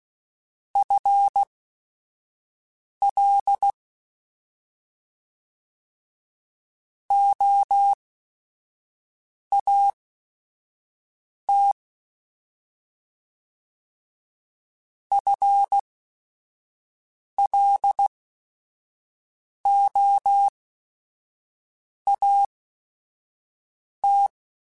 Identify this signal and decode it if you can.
Morse code, heard as a keyed tone, about 16 wpm, spaced as for 5 wpm, 800 Hz. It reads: FL OAT FLOAT
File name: FL / FL mp3s / Fast FL @5